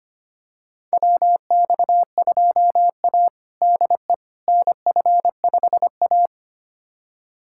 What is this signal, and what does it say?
Morse code, heard as a keyed tone, about 25 wpm, 700 Hz. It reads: WX2A DE NF5A
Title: USA Callsign DE Callsign - 4 Characters - 25wpm